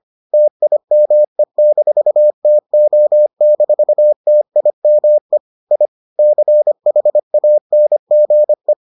Morse code, heard as a keyed tone, about 25 wpm, 600 Hz. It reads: TIME-TO-TIME I CHANGE